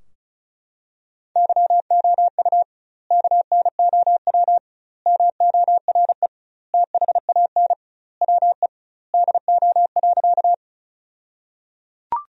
Morse code, heard as a keyed tone, about 36 wpm, 700 Hz. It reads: YOU KNOW MORE THAN WE DO.
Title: 2x Word-Spacing - Sentences from Top 100 Words - 35wpm